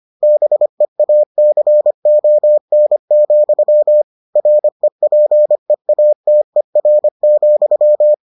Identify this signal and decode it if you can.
Morse code, heard as a keyed tone, about 25 wpm, 600 Hz. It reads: BEACON, REPEATER,